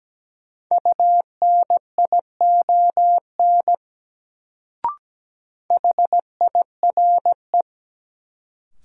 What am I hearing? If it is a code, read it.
Morse code, heard as a keyed tone, about 17 wpm, 700 Hz. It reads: UNION HIRE
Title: Sentences from Top 1400 Words - New Word Drill - 17wpm